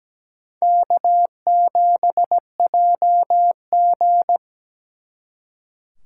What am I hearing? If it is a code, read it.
Morse code, heard as a keyed tone, about 20 wpm, 700 Hz. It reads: K7JG
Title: Speed Racing - USA Call Signs 17wpm